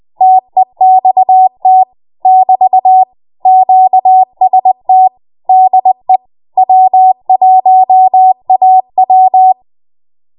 Morse code, heard as a keyed tone, about 20 wpm, 750 Hz. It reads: TEXT <BT> QST DE W1AW
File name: Morse / 20WPM